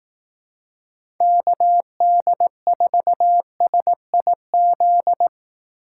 Morse code, heard as a keyed tone, about 18 wpm, 700 Hz. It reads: KD4SIZ
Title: Speed Racing - USA 2x3 Call Signs 15wpm